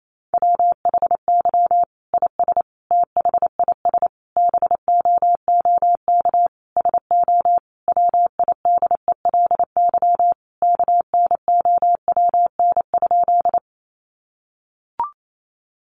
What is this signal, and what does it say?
Morse code, heard as a keyed tone, about 28 wpm, 700 Hz. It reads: WHY IS THIS BOOK SO WIDELY KNOWN?
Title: Sentences from Top 1200 Words 28wpm